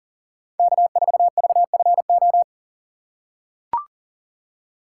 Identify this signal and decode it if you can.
Morse code, heard as a keyed tone, about 37 wpm, 700 Hz. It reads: K4VFO
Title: USA Call Signs - 3x Character Spacing - 40wpm